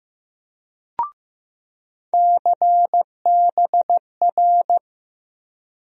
Morse code, heard as a keyed tone, about 15 wpm, 700 Hz. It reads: CBR
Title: USA Call Sign Postfixes - 3 Letters 15wpm